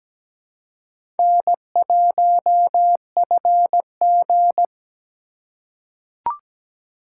Morse code, heard as a keyed tone, about 17 wpm, 700 Hz. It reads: N1FG
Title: USA Call Signs - 4 Characters - 2x Character Spacing - 17wpm